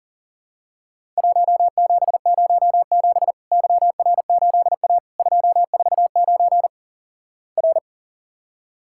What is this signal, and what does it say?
Morse code, heard as a keyed tone, about 43 wpm, 700 Hz. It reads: 1707 YR8A 249 R
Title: Worked All Europe DX Contest - QTC Exchange - 40wpm